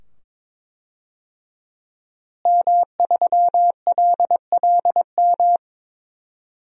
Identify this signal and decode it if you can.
Morse code, heard as a keyed tone, about 22 wpm, 700 Hz. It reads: M3LLM
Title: DXCC-100 Call Signs - 5 Characters - Repeated 2x - 22wpm